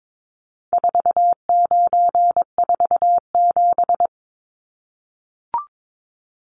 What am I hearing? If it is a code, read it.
Morse code, heard as a keyed tone, about 22 wpm, 700 Hz. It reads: 4947
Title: Numbers - 4-Digit 22wpm